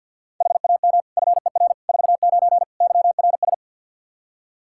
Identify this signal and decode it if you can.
Morse code, heard as a keyed tone, about 52 wpm, 700 Hz. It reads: HAM FER 49 YRS
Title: Practice QSO - Exchange 3 - More Info - 50wpm